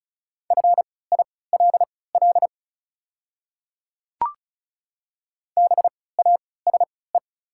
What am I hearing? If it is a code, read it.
Morse code, heard as a keyed tone, about 35 wpm, spaced as for 20 wpm, 700 Hz. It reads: FILL BASE
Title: 3x Character Spacing - Top 500 Words - 35wpm